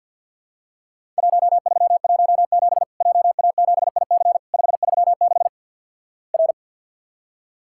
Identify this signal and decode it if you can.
Morse code, heard as a keyed tone, about 50 wpm, 700 Hz. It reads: 1317 JA7IC 536 R
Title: Worked All Europe DX Contest - QTC Exchange with Repeats - 50wpm